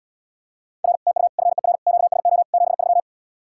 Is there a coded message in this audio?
Yes: Morse code from 0.8 to 3.0 s, heard as a keyed tone, about 80 wpm, 700 Hz. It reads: R TU FER QSO 73